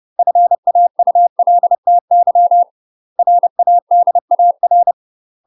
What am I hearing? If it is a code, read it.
Morse code, heard as a keyed tone, about 30 wpm, 700 Hz. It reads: FAULTY RADAR
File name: UK MASTER QSOs TRAINING / 30WPM / UK MASTER QSOs 0024